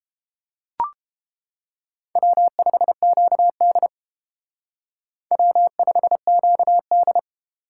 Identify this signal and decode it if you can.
Morse code, heard as a keyed tone, about 31 wpm, 700 Hz. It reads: W5QD W5QD